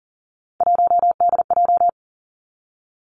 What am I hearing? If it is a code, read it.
Morse code, heard as a keyed tone, about 42 wpm, 700 Hz. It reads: 1DJ
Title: Three Challenging Letter-Number Combinations 40wpm